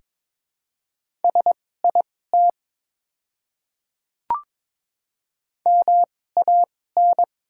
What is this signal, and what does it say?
Morse code, heard as a keyed tone, about 22 wpm, spaced as for 16 wpm, 690 Hz. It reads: SIT MAN
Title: 2x Character Spacing - Top 500 Words - 22wpm